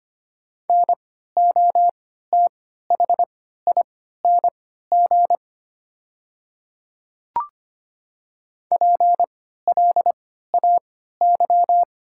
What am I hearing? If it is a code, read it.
Morse code, heard as a keyed tone, about 25 wpm, spaced as for 14 wpm, 690 Hz. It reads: NOTHING PLAY
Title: 3x Character Spacing - Top 500 Words - 25wpm